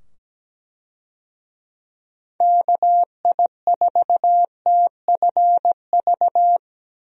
Morse code, heard as a keyed tone, about 17 wpm, 700 Hz. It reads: KI4TFV
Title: USA Call Signs - 6 Characters - 17wpm